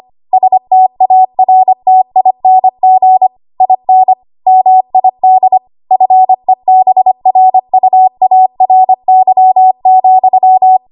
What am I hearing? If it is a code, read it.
Morse code, heard as a keyed tone, about 25 wpm, 750 Hz. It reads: STARTING IN MID FEBRUARY,
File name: Archive / 25 WPM / 25WPM